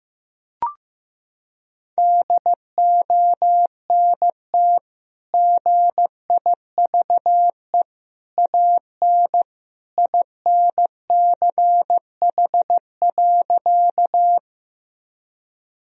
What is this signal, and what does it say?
Morse code, heard as a keyed tone, about 16 wpm, 700 Hz. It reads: DONT GIVE AN INCH.
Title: Sentences from Top 1000 Words - Review Part 3 - 15wpm